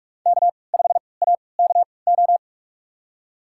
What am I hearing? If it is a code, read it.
Morse code, heard as a keyed tone, about 46 wpm, spaced as for 26 wpm, 700 Hz. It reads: K5AKO
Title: USA Call Signs - 5 Characters - 3x Character Spacing - 45wpm